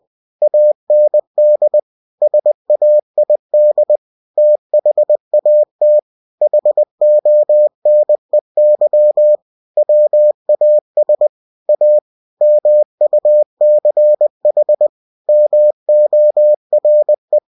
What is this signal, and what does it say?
Morse code, heard as a keyed tone, about 20 wpm, 600 Hz. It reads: AND SAID THAT HONEY WAS A MUCH MORE
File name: cw-books / winnie-the-pooh / winnie-20wpm-page-0037